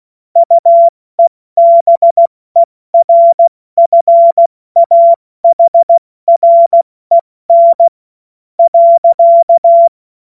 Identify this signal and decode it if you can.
Morse code, heard as a keyed tone, about 16 wpm, 680 Hz. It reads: UEBERFAHREN .
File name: Morsekurs / Witze / Witz 02 April 2024 70 F80